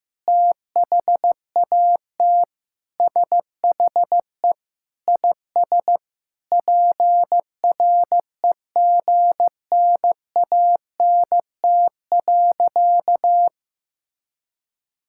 Morse code, heard as a keyed tone, about 15 wpm, 700 Hz. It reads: THAT SHE IS PREGNANT.
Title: Sentences from Top 1000 Words - Review Part 6 - 15wpm